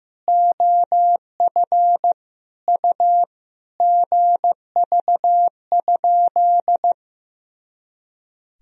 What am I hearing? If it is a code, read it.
Morse code, heard as a keyed tone, about 15 wpm, 700 Hz. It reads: OF U GV?
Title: Sentences from Top 200 Words and Abbreviations 15wpm